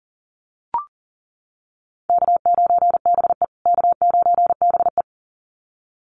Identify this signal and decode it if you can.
Morse code, heard as a keyed tone, about 40 wpm, 700 Hz. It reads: K9BE K9BE
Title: USA Call Signs Repeated 2x 40wpm